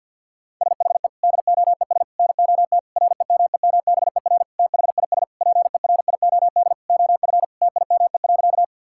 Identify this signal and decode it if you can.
Morse code, heard as a keyed tone, about 51 wpm, 690 Hz. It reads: SHE DOES NOT REMEMBER THIS PERIOD OF TIME.
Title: Sentences from Top 1000 Words - Review Part 5 - 50wpm